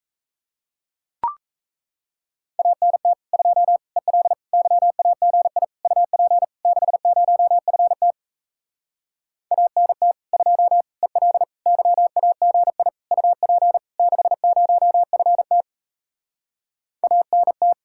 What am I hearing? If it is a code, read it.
Morse code, heard as a keyed tone, about 37 wpm, 700 Hz. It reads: ANT 2 EL YAGI UP 60FT ANT 2 EL YAGI UP 60FT ANT